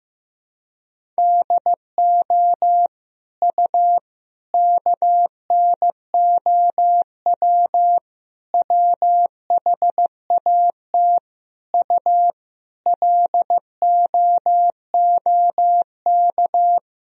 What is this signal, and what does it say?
Morse code, heard as a keyed tone, about 15 wpm, 700 Hz. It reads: DO U KNOW WHAT U LOOK
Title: Sentences from Top 100 Words and Abbreviations 15wpm